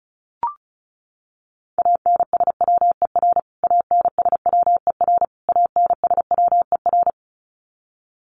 Morse code, heard as a keyed tone, about 34 wpm, 700 Hz. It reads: ANSWER ANSWER ANSWER
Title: Top 500 Words Repeated 35wpm